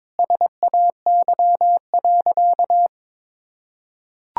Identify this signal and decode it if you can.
Morse code, heard as a keyed tone, about 22 wpm, 700 Hz. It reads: SAY.
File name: practice / 22wpm / Sentences with 4-Word Phrases - 22wpm